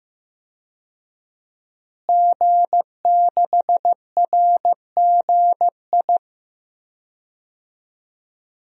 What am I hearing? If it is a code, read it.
Morse code, heard as a keyed tone, about 15 wpm, 700 Hz. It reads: G6RGI
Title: DXCC-100 Call Signs - 5 Characters - Repeated 2x - 15wpm